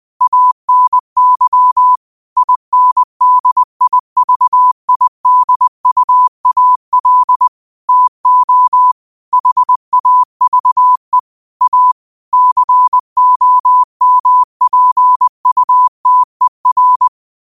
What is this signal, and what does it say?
Morse code, heard as a keyed tone, about 20 wpm, 1000 Hz. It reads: ANY INDIVIDUAL TO HAVE A COMPUTER